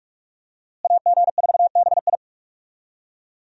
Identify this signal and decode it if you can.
Morse code, heard as a keyed tone, about 46 wpm, 700 Hz. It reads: AG4BI
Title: USA Call Signs - 5 Characters - Repeated 3x - 45wpm